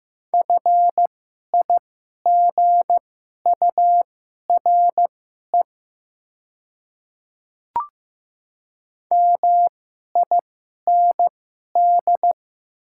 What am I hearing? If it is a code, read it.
Morse code, heard as a keyed tone, about 15 wpm, spaced as for 11 wpm, 700 Hz. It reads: FIGURE MIND
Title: 2x Character Spacing - Top 500 Words - 15wpm